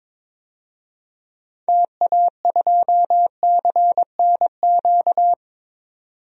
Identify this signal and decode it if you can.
Morse code, heard as a keyed tone, about 22 wpm, 700 Hz. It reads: TA2CNQ
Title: DXCC-100 Call Signs - 6 Characters - 22wpm